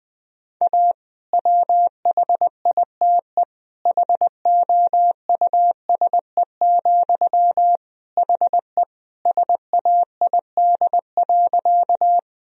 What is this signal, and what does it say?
Morse code, heard as a keyed tone, about 20 wpm, 700 Hz. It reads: A WHITE HOUSE, HE SAID.
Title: Sentences from Top 1000 Words - Review Part 1 - 20wpm